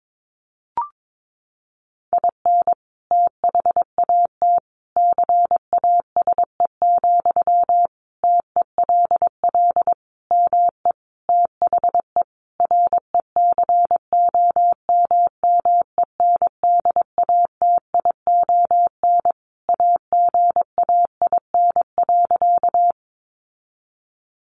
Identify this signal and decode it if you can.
Morse code, heard as a keyed tone, about 22 wpm, 700 Hz. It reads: IN THAT CASE, TELL ME THE RECOMMENDATION AGAIN.
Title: Sentences from Top 1400 Words 22wpm